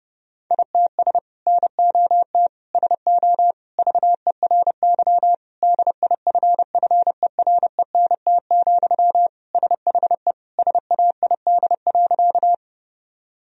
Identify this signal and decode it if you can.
Morse code, heard as a keyed tone, about 28 wpm, 700 Hz. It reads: ITS NOT SO VERY DIFFERENT, SHE SAID.
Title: Sentences from Top 1000 Words - Review Part 3 - 30wpm